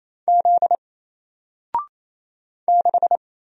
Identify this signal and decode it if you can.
Morse code, heard as a keyed tone, about 27 wpm, 700 Hz. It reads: Z 6
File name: Single-Letter-Number-28wpm